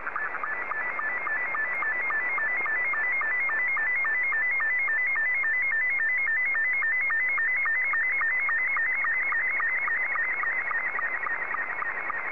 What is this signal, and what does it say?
SSTV (slow-scan television), Scottie 2.